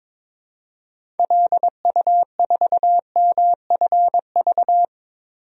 Morse code, heard as a keyed tone, about 22 wpm, 700 Hz. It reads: LU4MFV
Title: DXCC-100 Call Signs - 6 Characters - 22wpm